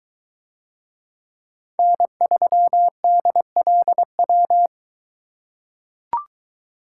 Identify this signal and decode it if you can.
Morse code, heard as a keyed tone, about 21 wpm, 700 Hz. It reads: N3DLW